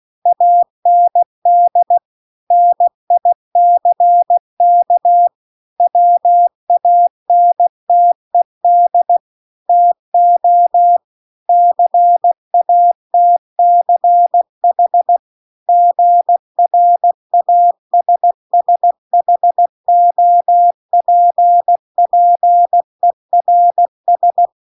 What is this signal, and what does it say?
Morse code, heard as a keyed tone, about 16 wpm, 700 Hz. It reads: AND NICK WANTED TO CATCH GRASSHOPPERS
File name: 16WPM